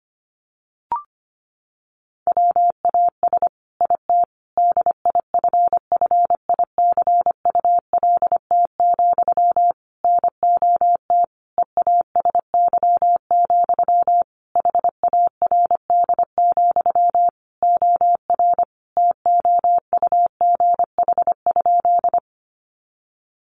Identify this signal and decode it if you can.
Morse code, heard as a keyed tone, about 25 wpm, 700 Hz. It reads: WAS IT DIFFICULT, NOT EASY, HARD, OR TOUGH?
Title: Sentences from Top 1200 Words 25wpm